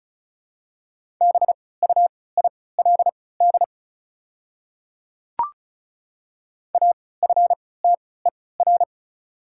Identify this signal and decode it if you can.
Morse code, heard as a keyed tone, about 36 wpm, spaced as for 20 wpm, 700 Hz. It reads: BUILD AFTER
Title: 3x Character Spacing - Top 200 Words - 35wpm